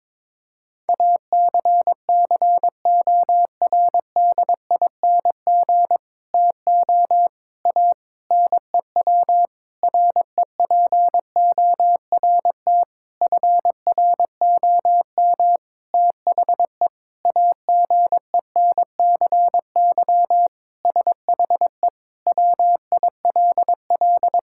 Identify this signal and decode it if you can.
Morse code, heard as a keyed tone, about 22 wpm, 700 Hz. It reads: ACCORDING TO A NEW REPORT FROM THE AGENCY SHE WILL
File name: Sentences with 7-Word Phrases - 22wpm